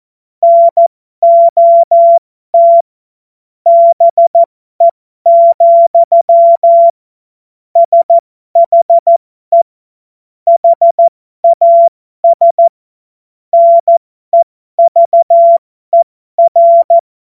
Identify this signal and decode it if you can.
Morse code, heard as a keyed tone, about 14 wpm, spaced as for 12 wpm, 680 Hz. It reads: NOT BE, SHE HAS NEVER